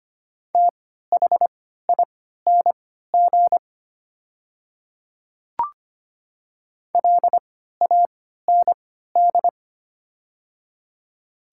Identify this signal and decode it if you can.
Morse code, heard as a keyed tone, about 25 wpm, spaced as for 14 wpm, 700 Hz. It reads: THING LAND
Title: 3x Character Spacing - Top 200 Words - 25wpm